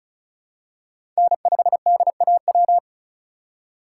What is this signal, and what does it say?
Morse code, heard as a keyed tone, about 36 wpm, 700 Hz. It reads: N5DAW
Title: USA Call Signs - 5 Characters - 3x Character Spacing - 35wpm